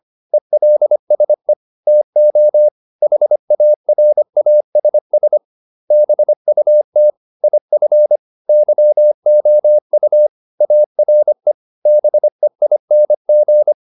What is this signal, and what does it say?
Morse code, heard as a keyed tone, about 25 wpm, 600 Hz. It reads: ELSE TO HARASS BUT IF YOU ARE BEING